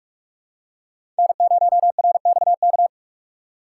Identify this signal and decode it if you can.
Morse code, heard as a keyed tone, about 44 wpm, 700 Hz. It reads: N0RXK
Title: USA Call Signs - 5 Characters - 2x Character Spacing - 45wpm